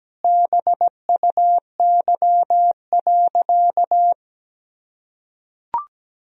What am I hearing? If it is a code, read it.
Morse code, heard as a keyed tone, about 17 wpm, 700 Hz. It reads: BUY.
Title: Sentences from Top 1000 Words - Review Part 4 - 17wpm